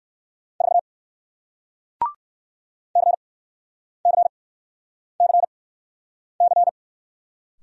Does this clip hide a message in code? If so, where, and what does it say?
Yes: Morse code from 0.6 to 6.7 s, heard as a keyed tone, about 60 wpm, spaced as for 9 wpm, 700 Hz. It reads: 4 CCCC